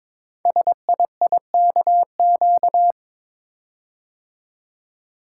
Morse code, heard as a keyed tone, about 22 wpm, 700 Hz. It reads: SIIKQ
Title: Code Groups - 5 Letters - 22wpm